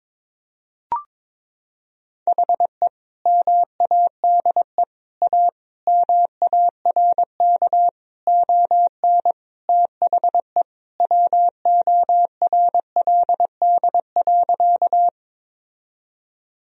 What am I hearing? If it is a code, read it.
Morse code, heard as a keyed tone, about 22 wpm, 700 Hz. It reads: HE MADE A MARK ON THE WORLD.
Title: Sentences from Top 1000 Words - Review Part 1 - 22wpm